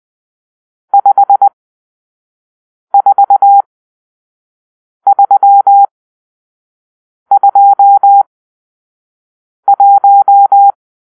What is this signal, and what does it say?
Morse code, heard as a keyed tone, about 20 wpm, spaced as for 5 wpm, 800 Hz. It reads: 54321
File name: arrl / Grp-6 Drill-2